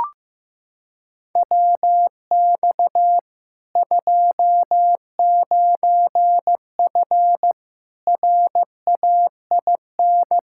Morse code, heard as a keyed tone, about 15 wpm, 700 Hz. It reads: WX 29F RAIN
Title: K1USN SST QSO Practice 15wpm